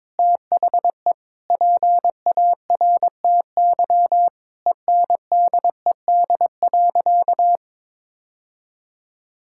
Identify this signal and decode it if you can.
Morse code, heard as a keyed tone, about 22 wpm, 700 Hz. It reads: THE PARTY ENDED.
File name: practice / 22wpm / Sentences with 2-Word Phrases - 22wpm